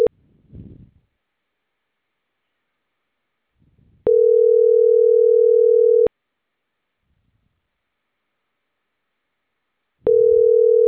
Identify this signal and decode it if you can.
Ringback tone (North America), 440 + 480 Hz.